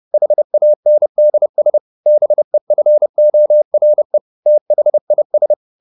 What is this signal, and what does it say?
Morse code, heard as a keyed tone, about 30 wpm, 600 Hz. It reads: HANDS BEFORE THIS
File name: the old man and the sea / the old man and the sea 1 30wpm